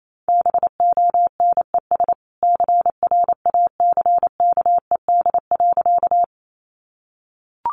Morse code, heard as a keyed tone, about 28 wpm, 700 Hz. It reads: BONES CRACKED.
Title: A Sound Of Thunder 28wpm